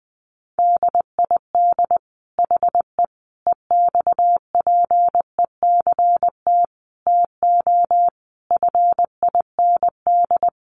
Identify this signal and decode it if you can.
Morse code, heard as a keyed tone, about 20 wpm, 700 Hz. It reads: DID HE EXPECT TO FIND
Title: Sentences from Top 1100 Words 20wpm